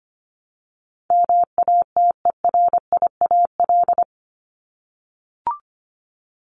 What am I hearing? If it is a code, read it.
Morse code, heard as a keyed tone, about 25 wpm, 700 Hz. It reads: MATERIAL
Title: ES:S Suffix - Top 100 Words with Base Words 25wpm